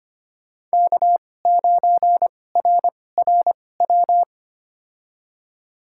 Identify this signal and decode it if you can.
Morse code, heard as a keyed tone, about 25 wpm, spaced as for 18 wpm, 700 Hz. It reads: K9RRW
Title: USA Call Signs - 5 Characters - 2x Character Spacing - 25wpm